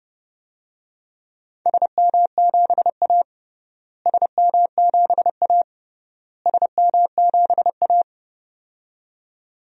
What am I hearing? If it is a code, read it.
Morse code, heard as a keyed tone, about 32 wpm, 700 Hz. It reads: SM7A SM7A SM7A
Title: DXCC-100 Call Signs - 4 Characters - Repeated 3x - 30wpm